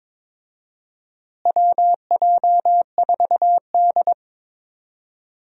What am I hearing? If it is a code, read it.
Morse code, heard as a keyed tone, about 22 wpm, 700 Hz. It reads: WJ4D
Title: USA Call Signs - 4 Characters - Repeated 3x - 22wpm